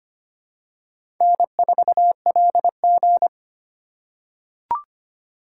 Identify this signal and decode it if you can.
Morse code, heard as a keyed tone, about 25 wpm, 700 Hz. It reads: N4LG
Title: USA Call Signs - 4 Characters - 3x Character Spacing - 25wpm